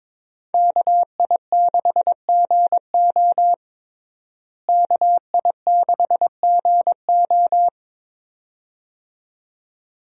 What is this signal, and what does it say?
Morse code, heard as a keyed tone, about 22 wpm, 700 Hz. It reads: KI6GO KI6GO
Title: USA Call Signs - 5 Characters - Repeated 2x - 22wpm